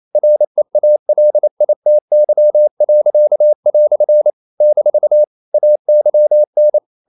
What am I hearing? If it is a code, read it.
Morse code, heard as a keyed tone, about 28 wpm, 600 Hz. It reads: REALITY." - AYN
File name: CW / Practice / EnglishQuotations0128